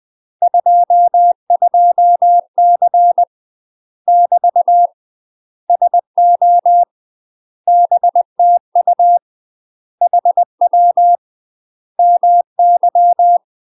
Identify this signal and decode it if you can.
Morse code, heard as a keyed tone, about 20 wpm, 700 Hz. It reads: 22C <BT> SO BTU HW MY
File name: UK MASTER QSOs TRAINING / 20WPM / UK MASTER QSOs 0001